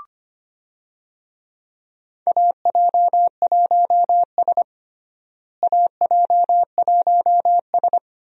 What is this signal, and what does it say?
Morse code, heard as a keyed tone, about 25 wpm, 700 Hz. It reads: AJ1S AJ1S
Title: USA Call Signs - 4 Characters - Repeated 2x - 25wpm